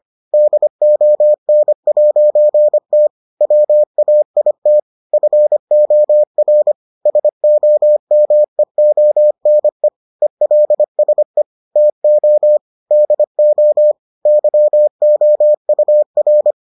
Morse code, heard as a keyed tone, about 25 wpm, 600 Hz. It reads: DON'T WAIT FOR SOMEONE ELSE TO DO YOUR